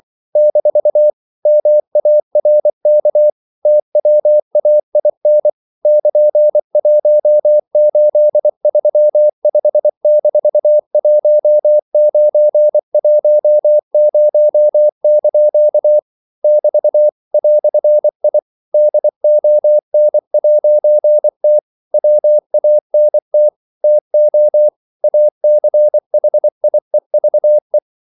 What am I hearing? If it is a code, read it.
Morse code, heard as a keyed tone, about 24 wpm, 600 Hz. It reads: - MARK TWAIN <KN>1835-1910) <BT> "I DON'T WANT TO ACHIEVE
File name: CW / Practice / EnglishQuotations0424